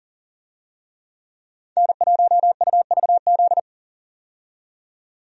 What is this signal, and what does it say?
Morse code, heard as a keyed tone, about 40 wpm, 700 Hz. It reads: N1UVZ
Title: USA Call Signs - 5 Characters - 40wpm